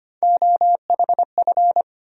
Morse code, heard as a keyed tone, about 25 wpm, 700 Hz. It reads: OHF
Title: USA Call Sign Postfixes - 3 Letters 25wpm